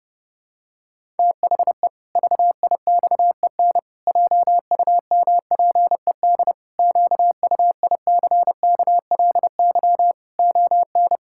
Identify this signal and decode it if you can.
Morse code, heard as a keyed tone, about 30 wpm, 700 Hz. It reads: THE VIXEN JUMPED QUICKLY ON
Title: Pangram Challenge v2 - 30wpm